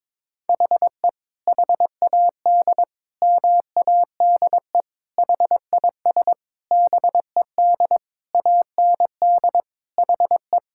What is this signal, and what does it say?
Morse code, heard as a keyed tone, about 22 wpm, 700 Hz. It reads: HE HAD MADE HIS BED AND HE
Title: Sentences from Top 1600 Words 22wpm